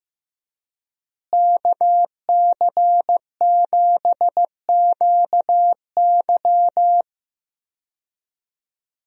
Morse code, heard as a keyed tone, about 15 wpm, 700 Hz. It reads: KC7QY
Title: USA Call Signs - 5 Characters - Repeated 2x - 15wpm